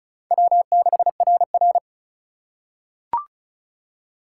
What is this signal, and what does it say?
Morse code, heard as a keyed tone, about 36 wpm, 700 Hz. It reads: W6RR
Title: USA Call Signs - 4 Characters - 4x Character Spacing - 35wpm